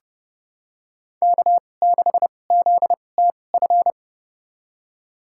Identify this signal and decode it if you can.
Morse code, heard as a keyed tone, about 30 wpm, spaced as for 22 wpm, 700 Hz. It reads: K6ZTF
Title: USA Call Signs - 5 Characters - 2x Character Spacing - 30wpm